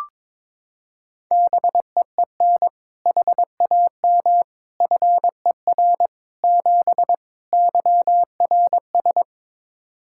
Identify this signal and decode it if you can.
Morse code, heard as a keyed tone, about 22 wpm, 700 Hz. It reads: BEEN HAM FER 7 YRS
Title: K1USN SST QSO Practice 22wpm